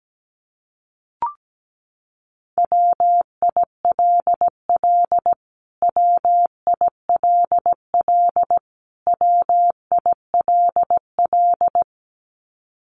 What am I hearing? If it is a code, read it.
Morse code, heard as a keyed tone, about 17 wpm, 700 Hz. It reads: WILL WILL WILL
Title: Top 200 Words Repeated 17wpm